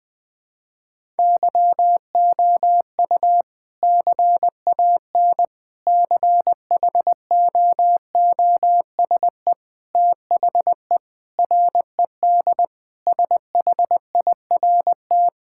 Morse code, heard as a keyed tone, about 20 wpm, 700 Hz. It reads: YOU CAN CHOOSE THE RED SHIRT
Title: Sentences with 2-Word Phrases - Spoken Beginning - 20wpm